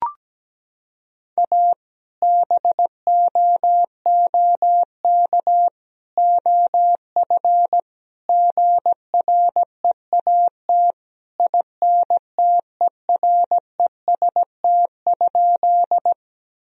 Morse code, heard as a keyed tone, about 17 wpm, 700 Hz. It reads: A BOOK OF GREAT INTEREST?
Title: Sentences from Top 1000 Words - Review Part 2 - 17wpm